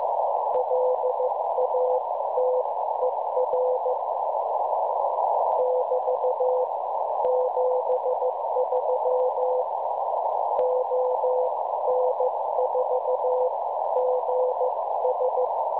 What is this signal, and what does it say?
Morse code, heard as a keyed tone, about 14 wpm, spaced as for 13 wpm, 510 Hz, with noise. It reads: LATER <BT> 73 ON4GS
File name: cwronde / websdr recording .6kHz